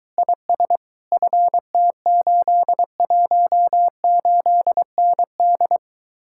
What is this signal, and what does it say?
Morse code, heard as a keyed tone, about 23 wpm, 700 Hz. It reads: IS FT818ND